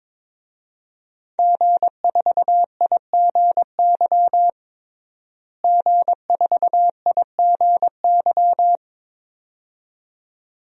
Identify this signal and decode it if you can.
Morse code, heard as a keyed tone, about 22 wpm, 700 Hz. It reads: G4IGY G4IGY
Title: DXCC-100 Call Signs - 5 Characters - Repeated 2x - 22wpm